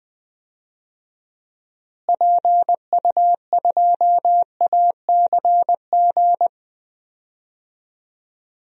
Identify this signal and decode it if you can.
Morse code, heard as a keyed tone, about 20 wpm, 700 Hz. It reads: PU2ACG
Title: DXCC-100 Call Signs - 6 Characters - Repeated 2x - 20wpm